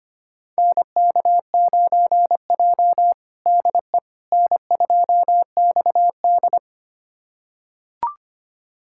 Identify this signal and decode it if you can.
Morse code, heard as a keyed tone, about 25 wpm, 700 Hz. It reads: NK9J DE N2XD
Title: USA Callsign DE Callsign - 4 Characters - 25wpm